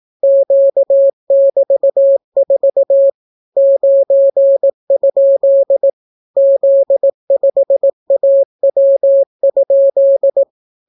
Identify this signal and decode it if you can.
Morse code, heard as a keyed tone, about 18 wpm, 550 Hz. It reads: Q<BT>4 9? Z5AW?